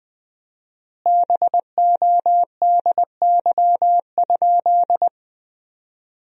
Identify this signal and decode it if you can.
Morse code, heard as a keyed tone, about 21 wpm, 700 Hz. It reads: BODY?
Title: 4x Word-Spacing - Sentences from Top 400 Words - 20wpm